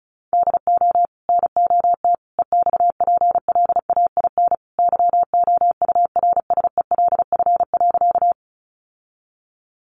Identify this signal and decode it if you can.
Morse code, heard as a keyed tone, about 34 wpm, 700 Hz. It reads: DO NOT EXPLAIN YOURSELF.
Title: Sentences from Top 1100 Words 35wpm